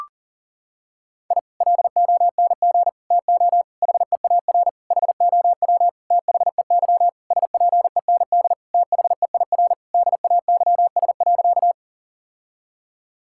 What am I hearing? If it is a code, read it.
Morse code, heard as a keyed tone, about 40 wpm, 700 Hz. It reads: I LONG TO HEAR HOW THEY SPEND THEIR DAYS.
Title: Sentences from Top 1000 Words - Review Part 5 - 40wpm